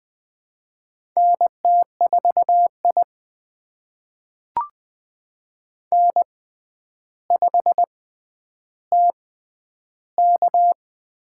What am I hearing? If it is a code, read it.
Morse code, heard as a keyed tone, about 20 wpm, spaced as for 10 wpm, 700 Hz. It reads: NT4I N5TK
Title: USA Call Signs - 4 Characters - 6x Character Spacing - 20wpm